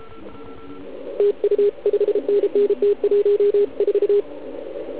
Morse code, heard as a keyed tone, about 33 wpm, 410 Hz, with radio noise. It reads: TU 5NN T14
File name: Archiv / CUC129 / CUC129 pileup